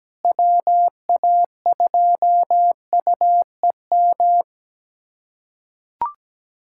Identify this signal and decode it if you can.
Morse code, heard as a keyed tone, about 17 wpm, 700 Hz. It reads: WA2UEM
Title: USA Callsign DE Callsign - 6 Characters - 17wpm